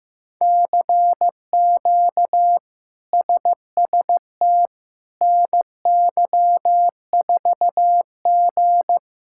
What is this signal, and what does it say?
Morse code, heard as a keyed tone, about 16 wpm, 700 Hz. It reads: CQ SST NY4G
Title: K1USN SST QSO Practice - 15wpm